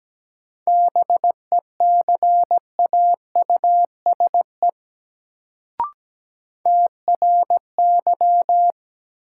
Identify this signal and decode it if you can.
Morse code, heard as a keyed tone, about 17 wpm, 700 Hz. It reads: BECAUSE TRY
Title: Sets of 2 Words - New Word Drill - 17wpm